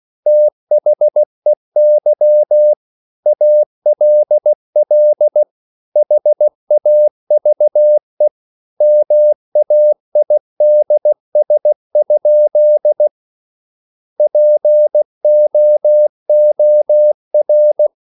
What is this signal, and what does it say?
Morse code, heard as a keyed tone, about 16 wpm, 600 Hz. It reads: THEY ALL HAVE MAIDS? POOR